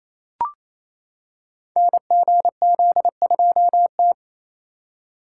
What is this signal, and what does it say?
Morse code, heard as a keyed tone, about 29 wpm, 700 Hz. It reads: NGZ2T
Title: Code Groups - 5 Letter:Numbers - 28wpm